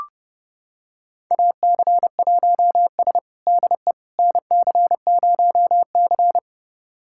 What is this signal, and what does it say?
Morse code, heard as a keyed tone, about 29 wpm, 700 Hz. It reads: AC1S DE NC0C
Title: USA Callsign DE Callsign - 4 Characters - 30wpm